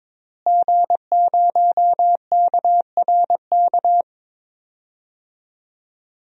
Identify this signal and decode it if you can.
Morse code, heard as a keyed tone, about 22 wpm, 700 Hz. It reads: G0KRK
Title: DXCC-100 Call Signs - 5 Characters - Repeated 3x - 22wpm